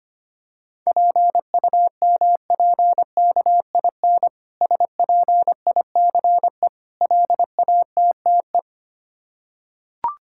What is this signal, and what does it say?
Morse code, heard as a keyed tone, about 25 wpm, 690 Hz. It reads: PUMPKIN SPICE LATTE
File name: practice / 25wpm / Difficult Words with Hints - 25wpm